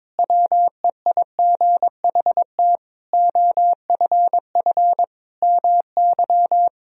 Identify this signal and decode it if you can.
Morse code, heard as a keyed tone, about 22 wpm, 700 Hz. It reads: WEIGHT OFF MY
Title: Sentences from Top 1000 Words - Review Part 3 - 22wpm